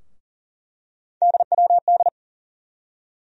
Morse code, heard as a keyed tone, about 40 wpm, 700 Hz. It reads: DWD
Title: USA Call Sign Postfixes - 3 Letters 40wpm